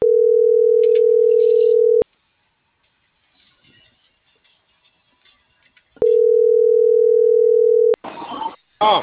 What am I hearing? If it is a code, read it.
Ringback tone (North America), 440 + 480 Hz.
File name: CENTRAL MAYAB / out jorge.armando